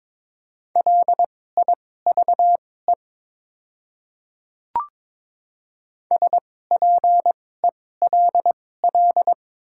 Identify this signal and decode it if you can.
Morse code, heard as a keyed tone, about 22 wpm, spaced as for 16 wpm, 700 Hz. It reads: LIVE SPELL
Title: 2x Character Spacing - Top 200 Words - 22wpm